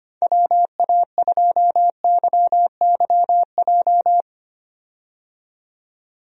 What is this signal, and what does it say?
Morse code, heard as a keyed tone, about 25 wpm, 700 Hz. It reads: WA2YYJ
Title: USA Call Signs - 6 Characters - 25wpm